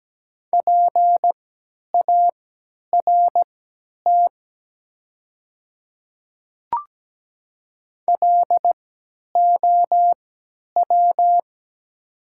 Morse code, heard as a keyed tone, about 17 wpm, spaced as for 10 wpm, 700 Hz. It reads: PART LOW
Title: 3x Character Spacing - Top 300 Words - 17wpm